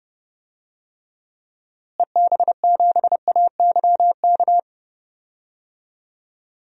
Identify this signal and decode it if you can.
Morse code, heard as a keyed tone, about 30 wpm, 700 Hz. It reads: EB7AYK
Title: DXCC-100 Call Signs - 6 Characters - Repeated 3x - 30wpm